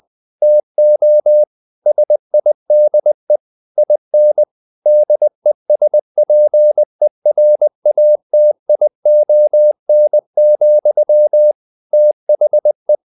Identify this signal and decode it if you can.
Morse code, heard as a keyed tone, about 20 wpm, 600 Hz. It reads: TO SIDE IN DESPERATION, THE
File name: the old man and the sea part 2 20wpm